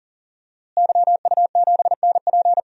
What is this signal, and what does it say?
Morse code, heard as a keyed tone, about 40 wpm, 700 Hz. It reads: YU7NP
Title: DXCC-100 Call Signs - 5 Characters - Repeated 2x - 40wpm